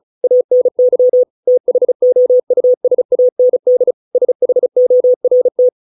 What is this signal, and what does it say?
Morse code, heard as a keyed tone, about 34 wpm, 500 Hz. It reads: ANY THOUSAND SHORT